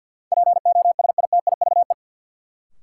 Morse code, heard as a keyed tone, about 49 wpm, 700 Hz. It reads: POSITIVE
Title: Sentences from Top 1000 Words - Review Part 5 - New Word Drill - 50wpm